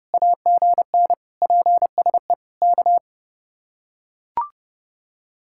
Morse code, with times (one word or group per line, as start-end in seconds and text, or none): 0.1-1.1 AGN
1.4-2.3 PSE
2.6-3.0 K